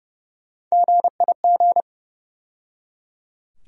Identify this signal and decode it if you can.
Morse code, heard as a keyed tone, about 30 wpm, 700 Hz. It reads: GIG
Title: USA Call Sign Postfixes - 3 Letters 30wpm